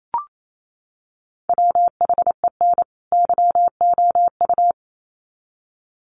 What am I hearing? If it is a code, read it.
Morse code, heard as a keyed tone, about 28 wpm, 700 Hz. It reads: WHEN YOU
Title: Sets of 2 Words 28wpm